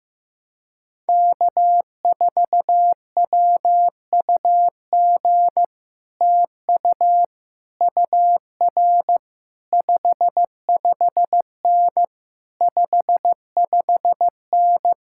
Morse code, heard as a keyed tone, about 15 wpm, 700 Hz. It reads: K4WUG TU UR 55N 55N
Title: Speed-Racing Common QSO Phrases Encore - 15wpm